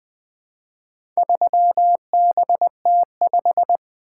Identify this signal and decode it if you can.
Morse code, heard as a keyed tone, about 20 wpm, 700 Hz. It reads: 3BT5
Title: Four Letter-Number Combinations - 20wpm